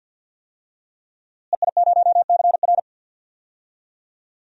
Morse code, heard as a keyed tone, about 52 wpm, 700 Hz. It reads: EI0CR
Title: DXCC-100 Call Signs - 5 Characters - Repeated 2x - 50wpm